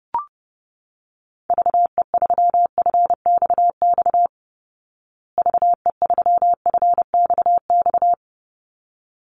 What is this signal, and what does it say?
Morse code, heard as a keyed tone, about 30 wpm, 700 Hz. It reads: VE3FXX VE3FXX
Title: Call Signs Repeated 3x - 30wpm